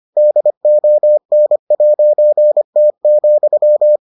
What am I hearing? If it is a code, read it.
Morse code, heard as a keyed tone, about 25 wpm, 600 Hz. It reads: DON'T,